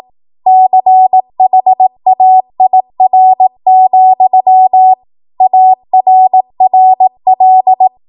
Morse code, heard as a keyed tone, about 18 wpm, 750 Hz. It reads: CHAIR, ARRL